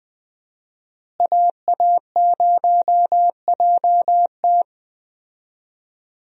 Morse code, heard as a keyed tone, about 20 wpm, 700 Hz. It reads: AA0JT
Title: USA Call Signs - 5 Characters - Repeated 2x - 20wpm